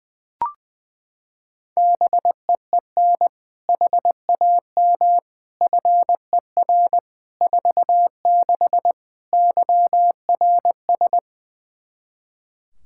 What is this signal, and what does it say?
Morse code, heard as a keyed tone, about 20 wpm, 700 Hz. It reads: BEEN HAM FER 46 YRS
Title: K1USN SST QSO Practice 20wpm